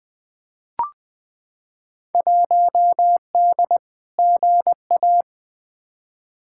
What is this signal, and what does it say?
Morse code, heard as a keyed tone, about 20 wpm, 700 Hz. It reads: 1D GA
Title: Field Day Exchanges - 20wpm